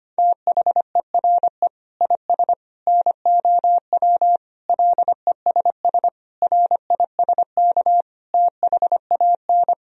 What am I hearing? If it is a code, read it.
Morse code, heard as a keyed tone, about 25 wpm, 700 Hz. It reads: THERE IS NOW LESS RISK THAN
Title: Sentences from Top 1000 Words - Review Part 4 - 25wpm